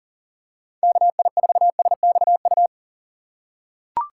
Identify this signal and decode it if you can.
Morse code, heard as a keyed tone, about 40 wpm, 700 Hz. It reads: KI4SXU
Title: USA Call Signs - 6 Characters - 3x Character Spacing - 40wpm